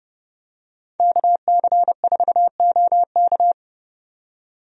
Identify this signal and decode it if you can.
Morse code, heard as a keyed tone, about 32 wpm, 700 Hz. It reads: KC4OK
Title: USA Call Signs - 5 Characters - 30wpm